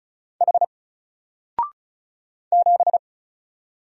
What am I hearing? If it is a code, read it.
Morse code, heard as a keyed tone, about 36 wpm, 700 Hz. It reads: H 7
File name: Single-Letter-Number-35wpm